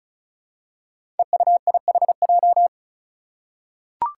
Morse code, heard as a keyed tone, about 36 wpm, 700 Hz. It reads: EUIHJ
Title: Five Challenging Letter-Number Combinations 35wpm